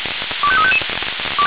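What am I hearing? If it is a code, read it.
Five-tone selective call (ZVEI), heard as quick runs of five tones, tones of about 70 ms. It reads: 25355
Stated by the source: Five-tone ZVEI selective call received by SDR.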